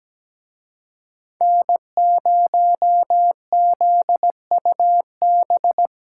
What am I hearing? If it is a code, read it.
Morse code, heard as a keyed tone, about 20 wpm, 700 Hz. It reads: N0ZUB
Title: Speed Racing - USA 1x3 Call Signs 17wpm